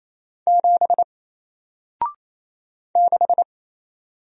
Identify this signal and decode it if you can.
Morse code, heard as a keyed tone, about 28 wpm, 700 Hz. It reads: 7 6
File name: Single-Letter-Number-28wpm